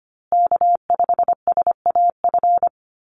Morse code, heard as a keyed tone, about 25 wpm, 700 Hz. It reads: K5SAF
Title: USA Call Signs Repeated 2x 25wpm